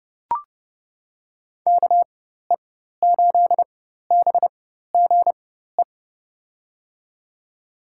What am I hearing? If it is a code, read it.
Morse code, heard as a keyed tone, about 32 wpm, spaced as for 14 wpm, 700 Hz. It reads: KE8BGE
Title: USA Call Signs - 6 Characters - 4x Character Spacing - 30wpm